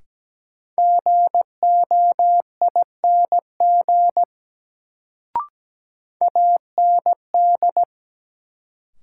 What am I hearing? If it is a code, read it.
Morse code, heard as a keyed tone, about 17 wpm, 700 Hz. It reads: GOING AND
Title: Sets of 2 Words - New Word Drill - 17wpm